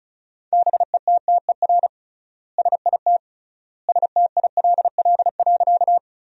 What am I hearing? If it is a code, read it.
Morse code, heard as a keyed tone, about 36 wpm, 700 Hz. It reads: BETTER SIT STILL.
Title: 3x Word-Spacing - Sentences from Top 500 Words - 35wpm